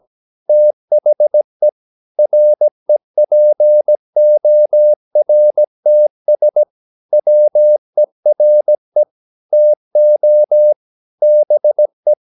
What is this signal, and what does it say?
Morse code, heard as a keyed tone, about 17 wpm, 600 Hz. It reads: THE REPORTS WERE TO BE
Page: Romanen Never af Ken Follet på engelsk.